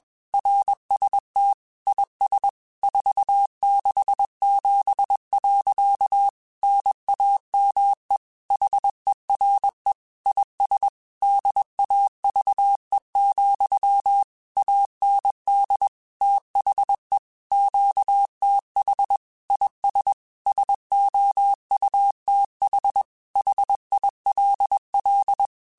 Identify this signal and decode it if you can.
Morse code, heard as a keyed tone, about 21 wpm, 800 Hz. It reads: RST IS 467. NAME HERE IS DAVE, AND THE QTH IS SOUTH HILL
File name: QSOs / QSOs-mp3s / kc7sac@20